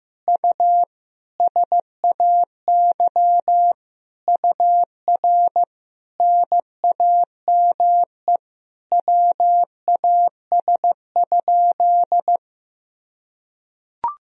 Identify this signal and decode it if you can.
Morse code, heard as a keyed tone, about 15 wpm, 700 Hz. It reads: U SAY UR NAME WAS?
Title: Sentences from Top 200 Words and Abbreviations 15wpm